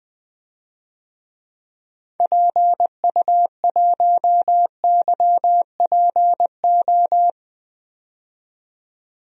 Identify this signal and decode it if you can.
Morse code, heard as a keyed tone, about 20 wpm, 700 Hz. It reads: PU1YPO
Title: DXCC-100 Call Signs - 6 Characters - 20wpm